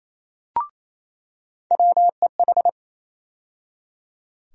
Morse code, heard as a keyed tone, about 27 wpm, 700 Hz. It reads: WEH
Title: USA Call Sign Postfixes - 3 Letters 28wpm